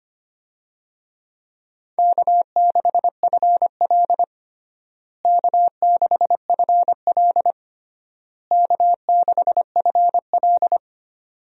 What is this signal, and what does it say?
Morse code, heard as a keyed tone, about 25 wpm, 700 Hz. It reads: K6FL K6FL K6FL
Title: USA Call Signs - 4 Characters - Repeated 3x - 25wpm